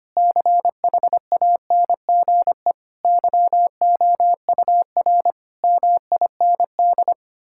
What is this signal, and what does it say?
Morse code, heard as a keyed tone, about 25 wpm, 700 Hz. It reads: CHANGE YOUR MIND
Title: Sentences from Top 1000 Words - Review Part 6 - 25wpm